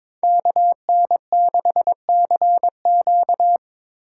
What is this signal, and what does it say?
Morse code, heard as a keyed tone, about 26 wpm, 700 Hz. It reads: KN6CQ
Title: Speed Racing - USA 2x2 Call Signs 22wpm